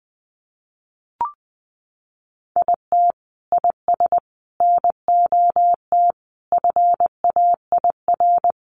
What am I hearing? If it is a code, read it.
Morse code, heard as a keyed tone, about 19 wpm, 700 Hz. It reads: IT IS NOT FAIR
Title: Sentences from Top 1300 Words 20wpm